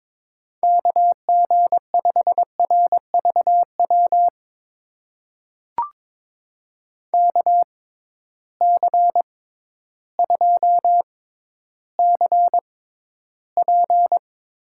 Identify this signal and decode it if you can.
Morse code, heard as a keyed tone, about 22 wpm, spaced as for 12 wpm, 700 Hz. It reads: KG5RVW KC2CP
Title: USA Call Signs - 6 Characters - 6x Character Spacing - 22wpm